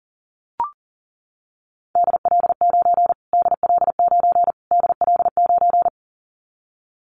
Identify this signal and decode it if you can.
Morse code, heard as a keyed tone, about 37 wpm, 700 Hz. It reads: DL9 DL9 DL9
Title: Callsign Prefixes for Top 25 DXCC - Repeated 3x - 40wpm